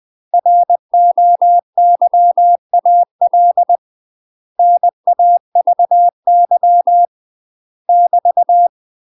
Morse code, heard as a keyed tone, about 20 wpm, 700 Hz. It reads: ROYAL NAVY <BT>
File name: UK MASTER QSOs TRAINING / 20WPM / UK MASTER QSOs 0029